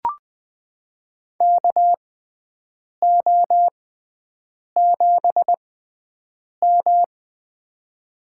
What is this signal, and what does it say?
Morse code, heard as a keyed tone, about 20 wpm, spaced as for 7 wpm, 700 Hz. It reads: KO7M
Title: USA Call Signs - 4 Characters - 6x Character Spacing - 20wpm